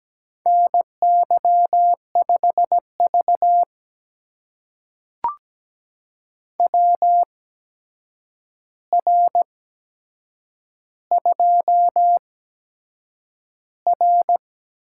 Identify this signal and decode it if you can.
Morse code, heard as a keyed tone, about 17 wpm, spaced as for 7 wpm, 700 Hz. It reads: NY5V WR2R
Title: USA Call Signs - 4 Characters - 8x Character Spacing - 17wpm